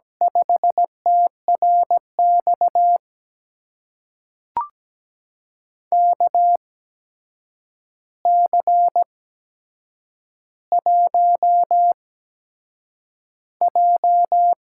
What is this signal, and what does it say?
Morse code, heard as a keyed tone, about 17 wpm, spaced as for 7 wpm, 700 Hz. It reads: 5TRX KC1J
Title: USA Call Signs - 8x Character Spacing - 17wpm